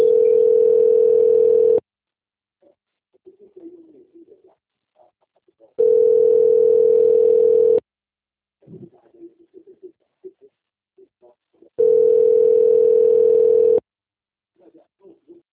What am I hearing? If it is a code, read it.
Ringback tone (North America), 440 + 480 Hz.
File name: out- - - - - .26520